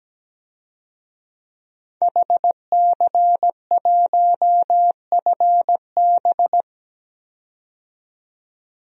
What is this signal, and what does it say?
Morse code, heard as a keyed tone, about 17 wpm, 700 Hz. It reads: HC1FB
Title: DXCC-100 Call Signs - 5 Characters - 17wpm